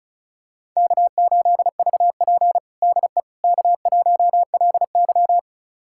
Morse code, heard as a keyed tone, about 36 wpm, 700 Hz. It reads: K8VP DE K1LY
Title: USA Callsign DE Callsign - 4 Characters - 35wpm